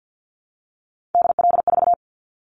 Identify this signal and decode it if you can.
Morse code, heard as a keyed tone, about 50 wpm, 700 Hz. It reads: DL4
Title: Callsign Prefixes for Top 25 DXCC - 50wpm